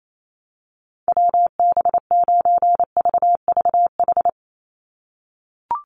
Morse code, heard as a keyed tone, about 28 wpm, 700 Hz. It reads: WB9VVH
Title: USA Call Signs 28wpm